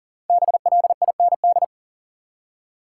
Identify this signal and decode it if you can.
Morse code, heard as a keyed tone, about 40 wpm, 700 Hz. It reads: BLIND
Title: Five Letter Words - 40wpm